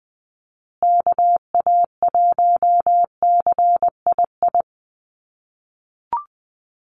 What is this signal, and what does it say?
Morse code, heard as a keyed tone, about 20 wpm, 700 Hz. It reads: KA1CII
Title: USA Call Signs Repeated 3x 20wpm